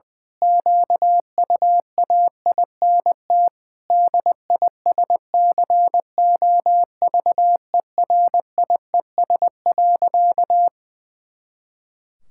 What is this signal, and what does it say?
Morse code, heard as a keyed tone, about 20 wpm, 700 Hz. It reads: QUAINT DISCOVERIES.
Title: Pangram Challenge v2 - 20wpm